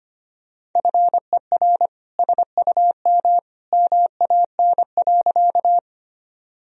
Morse code, heard as a keyed tone, about 25 wpm, 700 Hz. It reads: FER SUM MAN.
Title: Sentences from Top 200 Words and Abbreviations 25wpm